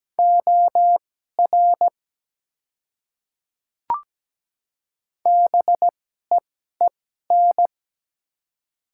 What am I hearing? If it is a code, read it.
Morse code, heard as a keyed tone, about 17 wpm, spaced as for 12 wpm, 700 Hz. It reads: OR BEEN
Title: 2x Character Spacing - Top 100 Words - 17wpm